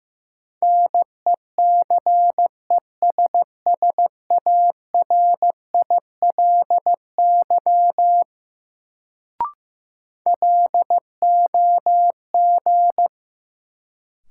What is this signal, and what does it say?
Morse code, heard as a keyed tone, about 15 wpm, 700 Hz. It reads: NECESSARILY LOG
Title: Sentences from Top 1400 Words - New Word Drill - 15wpm